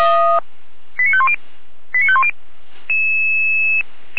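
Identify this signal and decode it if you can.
Five-tone selective call (ZVEI), heard as quick runs of five tones, tones of about 70 ms. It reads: 89510 89510
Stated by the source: ZVEI five-tone call 89095 received by SDR.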